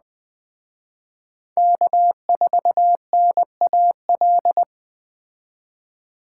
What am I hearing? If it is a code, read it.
Morse code, heard as a keyed tone, about 19 wpm, 700 Hz. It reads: K4NAL